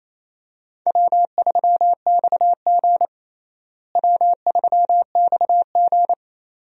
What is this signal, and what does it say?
Morse code, heard as a keyed tone, about 28 wpm, 700 Hz. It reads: W3XG W3XG
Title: USA Call Signs - 4 Characters - Repeated 2x - 28wpm